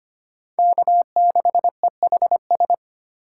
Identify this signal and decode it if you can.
Morse code, heard as a keyed tone, about 33 wpm, 700 Hz. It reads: K6EHS